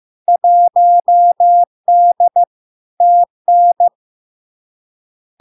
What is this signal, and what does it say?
Morse code, heard as a keyed tone, about 15 wpm, 700 Hz. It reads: 1D TN
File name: LARK-FD-2022-Log-in-CW-15wpm-700-Hz